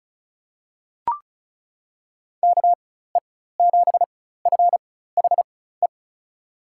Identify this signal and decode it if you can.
Morse code, heard as a keyed tone, about 34 wpm, spaced as for 16 wpm, 700 Hz. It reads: KE7FHE
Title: USA Call Signs - 4x Character Spacing - 35wpm